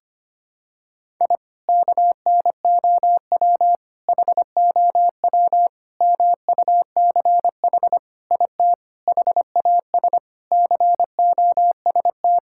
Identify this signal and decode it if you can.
Morse code, heard as a keyed tone, about 25 wpm, 700 Hz. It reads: I KNOW HOW MUCH IT HAS COST
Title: Sentences from Top 1000 Words - Review Part 5 - 25wpm